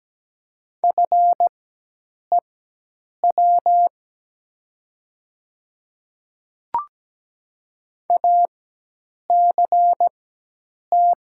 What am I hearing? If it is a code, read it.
Morse code, heard as a keyed tone, about 17 wpm, spaced as for 8 wpm, 700 Hz. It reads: FEW ACT
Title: 4x Character Spacing - Top 300 Words - 17wpm